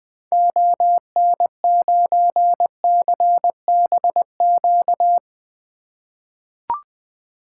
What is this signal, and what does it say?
Morse code, heard as a keyed tone, about 20 wpm, 700 Hz. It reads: ON9CBQ
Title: 2023 SOTA Activator Call Signs - 20wpm